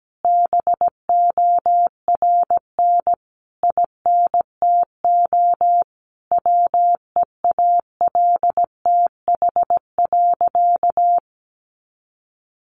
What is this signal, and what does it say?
Morse code, heard as a keyed tone, about 17 wpm, 700 Hz. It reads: BORN INTO WEALTH.
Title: Sentences from Top 1400 Words 17wpm